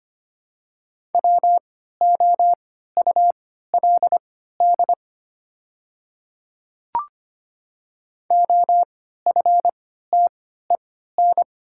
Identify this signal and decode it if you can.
Morse code, heard as a keyed tone, about 25 wpm, spaced as for 14 wpm, 700 Hz. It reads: WOULD OFTEN
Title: 3x Character Spacing - Top 300 Words - 25wpm